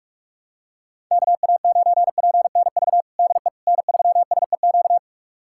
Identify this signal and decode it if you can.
Morse code, heard as a keyed tone, about 45 wpm, 700 Hz. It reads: KA9PNV DE N3SEQ
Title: USA Callsign DE Callsign 45wpm